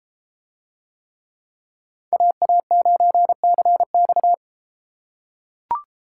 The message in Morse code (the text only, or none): AA9CX